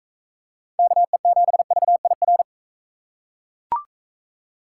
Morse code, heard as a keyed tone, about 42 wpm, 700 Hz. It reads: KE7VIR